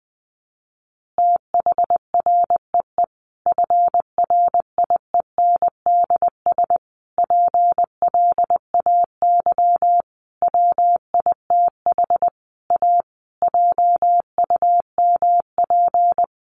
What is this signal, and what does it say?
Morse code, heard as a keyed tone, about 20 wpm, 700 Hz. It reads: THREE FRIENDS PLAY WITH A JUMP
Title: Easy Phrases 20wpm